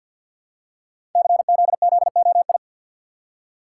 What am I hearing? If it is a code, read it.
Morse code, heard as a keyed tone, about 47 wpm, 700 Hz. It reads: CZZOI
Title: Code Groups - 5 Letters - 50wpm